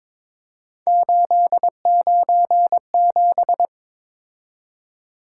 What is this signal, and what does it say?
Morse code, heard as a keyed tone, about 22 wpm, 700 Hz. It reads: 897
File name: practice / 22wpm / Three Letter-Number Combinations - 22wpm